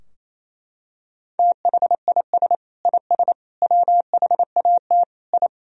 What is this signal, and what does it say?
Morse code, heard as a keyed tone, about 28 wpm, 700 Hz. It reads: THIS IS WHAT I
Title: Sentences from Top 1000 Words - Review Part 4 - 28wpm